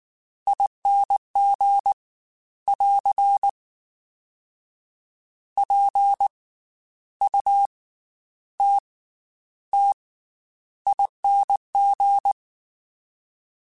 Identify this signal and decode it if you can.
Morse code, heard as a keyed tone, about 19 wpm, 800 Hz. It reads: ING <AR> P U T T ING
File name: Special INGs / Special INGs-mp3s / Slow ing @18